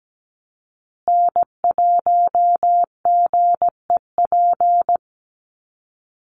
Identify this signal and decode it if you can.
Morse code, heard as a keyed tone, about 17 wpm, 700 Hz. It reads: N1GEP
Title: USA Call Signs Repeated 3x 17wpm